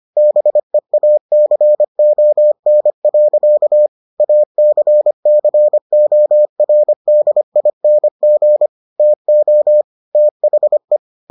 Morse code, heard as a keyed tone, about 25 wpm, 600 Hz. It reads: BEACON. ACCORDING TO THE